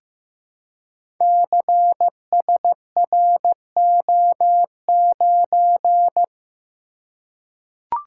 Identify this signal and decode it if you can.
Morse code, heard as a keyed tone, about 15 wpm, 700 Hz. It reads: CSRO9
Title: Code Groups - 5 Letter:Numbers - 15wpm